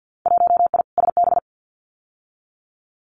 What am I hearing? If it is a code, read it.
Morse code, heard as a keyed tone, about 50 wpm, 700 Hz. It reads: 1I SB
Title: Winter Field Day Exchanges 50wpm